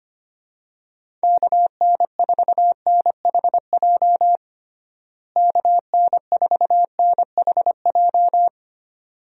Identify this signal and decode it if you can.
Morse code, heard as a keyed tone, about 25 wpm, 700 Hz. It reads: KN4NHJ KN4NHJ
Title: USA Call Signs - 6 Characters - Repeated 2x - 25wpm